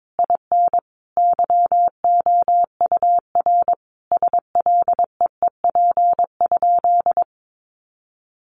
Morse code, heard as a keyed tone, about 22 wpm, 700 Hz. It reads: IN YOUR SLEEP?
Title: Sentences from Top 500 Words 22wpm